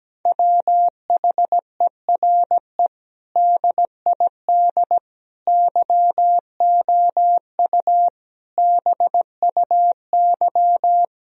Morse code, heard as a keyed tone, about 17 wpm, 700 Hz. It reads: WHERE DID YOU BUY
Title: Sentences from Top 1000 Words - Review Part 4 - 17wpm